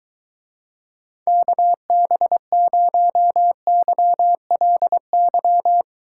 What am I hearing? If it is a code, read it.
Morse code, heard as a keyed tone, about 23 wpm, 700 Hz. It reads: KB0YLY